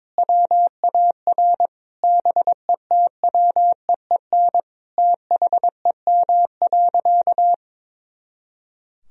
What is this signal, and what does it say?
Morse code, heard as a keyed tone, about 22 wpm, 700 Hz. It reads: WAR BETWEEN THEM.
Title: Sentences from Top 400 Words and Abbreviations 22wpm